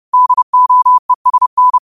Morse code, heard as a keyed tone, about 30 wpm, 1000 Hz. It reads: DOESN
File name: qotdfiles / 30wpm